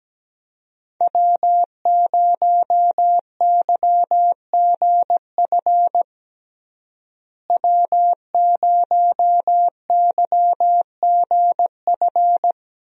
Morse code, heard as a keyed tone, about 17 wpm, 700 Hz. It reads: W0YGF W0YGF
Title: USA Call Signs - 5 Characters - Repeated 2x - 17wpm